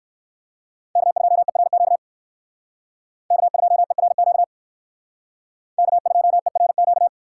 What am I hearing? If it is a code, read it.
Morse code, heard as a keyed tone, about 67 wpm, 700 Hz. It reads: K2ERQ K2ERQ K2ERQ